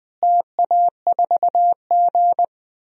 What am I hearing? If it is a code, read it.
Morse code, heard as a keyed tone, about 20 wpm, 700 Hz. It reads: TA4G
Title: DXCC-100 Call Signs - 4 Characters - Repeated 2x - 20wpm